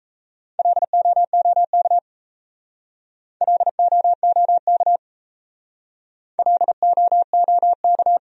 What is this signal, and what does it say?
Morse code, heard as a keyed tone, about 34 wpm, 700 Hz. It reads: LOOK LOOK LOOK
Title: Speed-Racing Top 300 Words 28wpm